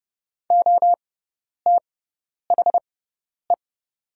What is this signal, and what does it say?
Morse code, heard as a keyed tone, about 29 wpm, spaced as for 10 wpm, 690 Hz. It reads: OTHE
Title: 6x Character Spacing - Top 300 Words - 30wpm